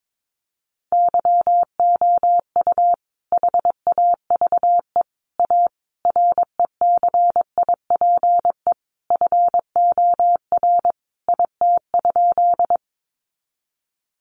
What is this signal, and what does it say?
Morse code, heard as a keyed tone, about 22 wpm, 700 Hz. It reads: YOU HAVE A RECIPE FOR IT?
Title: Sentences from Top 1200 Words 22wpm